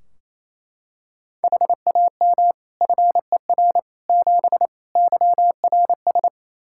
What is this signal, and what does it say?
Morse code, heard as a keyed tone, about 28 wpm, 700 Hz. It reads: HAM FER 7 YRS
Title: Practice QSO - Exchange 3 - More Info - 28wpm